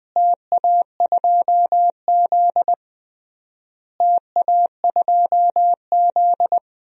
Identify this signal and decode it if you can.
Morse code, heard as a keyed tone, about 21 wpm, 700 Hz. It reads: TA2Z TA2Z
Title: DXCC-100 Call Signs - 4 Characters - Repeated 3x - 20wpm